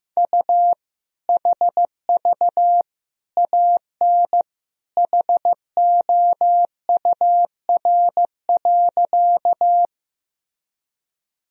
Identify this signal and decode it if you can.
Morse code, heard as a keyed tone, about 15 wpm, 700 Hz. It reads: U HV AN HOUR.
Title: Sentences from Top 400 Words and Abbreviations 15wpm